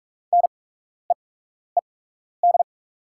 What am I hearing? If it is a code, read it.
Morse code, heard as a keyed tone, about 43 wpm, spaced as for 12 wpm, 690 Hz. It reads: NEED
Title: 8x Character Spacing - Top 200 Words - 45wpm